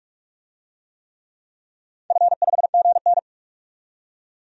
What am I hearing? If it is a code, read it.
Morse code, heard as a keyed tone, about 46 wpm, 700 Hz. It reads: F5GN
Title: DXCC-100 Call Signs - 4 Characters - 45wpm